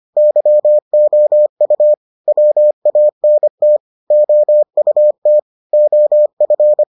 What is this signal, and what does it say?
Morse code, heard as a keyed tone, about 25 wpm, 600 Hz. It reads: YOU WANT OUT OF